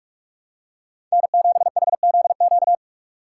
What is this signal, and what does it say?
Morse code, heard as a keyed tone, about 46 wpm, 700 Hz. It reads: N7HZQ
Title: USA Call Signs - 4x Character Spacing - 45wpm